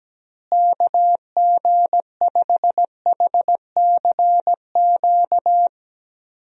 Morse code, heard as a keyed tone, about 17 wpm, 700 Hz. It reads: KG5HCQ
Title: USA Call Signs - 6 Characters - Repeated 2x - 17wpm